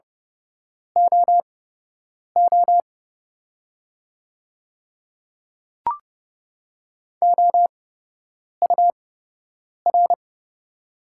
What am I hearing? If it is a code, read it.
Morse code, heard as a keyed tone, about 32 wpm, spaced as for 8 wpm, 700 Hz. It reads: OO OUR
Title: 8x Character Spacing - Top 300 Words - 30wpm